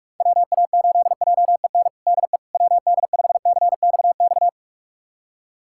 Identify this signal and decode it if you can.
Morse code, heard as a keyed tone, about 41 wpm, 700 Hz. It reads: WA8JEN DE WD5CXX
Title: USA Callsign DE Callsign 45wpm